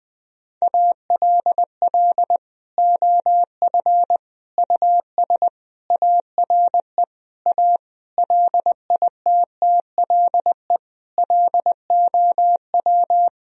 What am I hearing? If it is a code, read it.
Morse code, heard as a keyed tone, about 20 wpm, 700 Hz. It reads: ALL OF US ARE A LITTLE LOW
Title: Sentences from Top 1000 Words - Review Part 3 - 20wpm